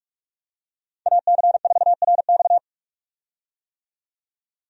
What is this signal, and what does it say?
Morse code, heard as a keyed tone, about 46 wpm, 700 Hz. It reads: AC4RX
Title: USA Call Signs - 5 Characters - 45wpm